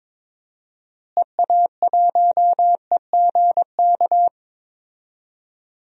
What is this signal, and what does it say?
Morse code, heard as a keyed tone, about 22 wpm, 700 Hz. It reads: EA1EGK
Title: DXCC-100 Call Signs - 6 Characters - Repeated 2x - 22wpm